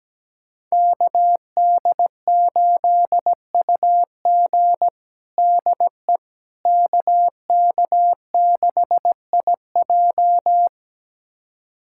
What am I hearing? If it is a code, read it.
Morse code, heard as a keyed tone, about 17 wpm, 700 Hz. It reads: KD8UG DE KK6IJ
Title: USA Callsign DE Callsign - 5 Characters - 17wpm